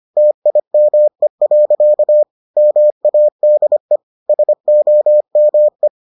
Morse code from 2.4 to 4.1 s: MADE

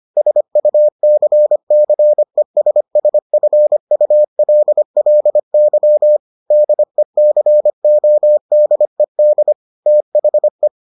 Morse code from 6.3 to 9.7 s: DECODED

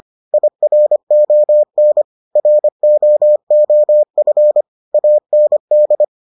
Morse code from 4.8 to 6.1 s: AND

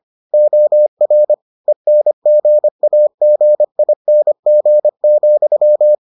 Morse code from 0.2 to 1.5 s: OR

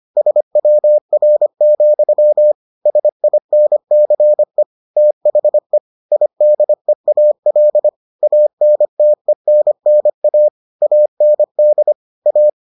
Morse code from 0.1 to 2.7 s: SWR,

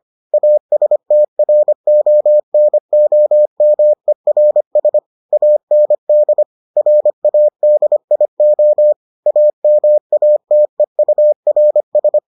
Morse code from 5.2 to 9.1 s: AND RADIO

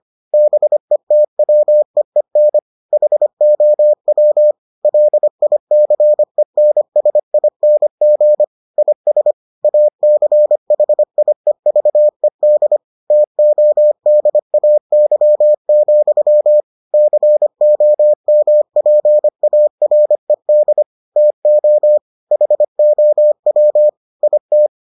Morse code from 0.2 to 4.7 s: BETWEEN HOW

